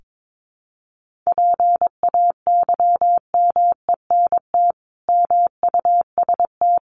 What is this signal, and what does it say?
Morse code, heard as a keyed tone, about 22 wpm, 700 Hz. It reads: PAYMENT MUST
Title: Sentences from Top 1500 Words 22wpm